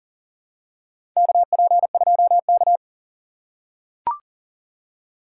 Morse code, heard as a keyed tone, about 40 wpm, 700 Hz. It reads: KP2K
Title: USA Call Signs - 4 Characters - 2x Character Spacing - 40wpm